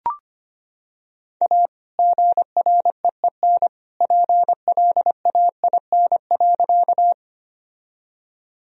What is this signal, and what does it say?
Morse code, heard as a keyed tone, about 25 wpm, 700 Hz. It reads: A GREEN PLAIN.
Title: Sentences from Top 1000 Words - Review Part 2 - 25wpm